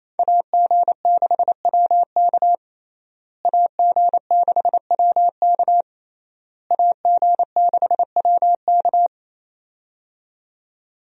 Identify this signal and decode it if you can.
Morse code, heard as a keyed tone, about 27 wpm, 700 Hz. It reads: AG6WK AG6WK AG6WK
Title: USA Call Signs - 5 Characters - Repeated 3x - 28wpm